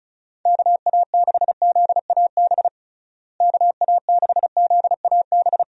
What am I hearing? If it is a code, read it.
Morse code, heard as a keyed tone, about 36 wpm, 700 Hz. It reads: KA6ZAB KA6ZAB
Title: USA Call Signs - 6 Characters - Repeated 2x - 35wpm